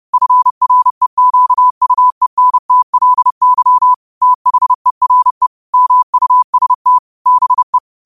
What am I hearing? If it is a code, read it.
Morse code, heard as a keyed tone, about 30 wpm, 1000 Hz. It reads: FREQUENTLY THERE MUST BE